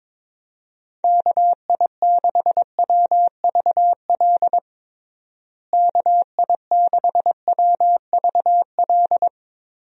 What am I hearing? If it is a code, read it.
Morse code, heard as a keyed tone, about 22 wpm, 700 Hz. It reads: KI6WVL KI6WVL
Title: USA Call Signs - 6 Characters - Repeated 3x - 22wpm